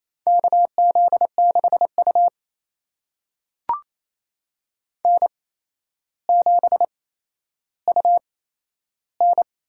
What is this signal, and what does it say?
Morse code, heard as a keyed tone, about 28 wpm, spaced as for 12 wpm, 700 Hz. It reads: KZ6U N7UN
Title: USA Call Signs - 4 Characters - 8x Character Spacing - 28wpm